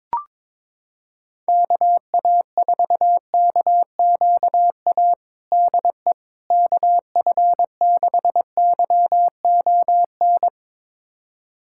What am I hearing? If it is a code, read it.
Morse code, heard as a keyed tone, about 22 wpm, 700 Hz. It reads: KA4KQA DE KF6YON
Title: USA Callsign DE Callsign 22wpm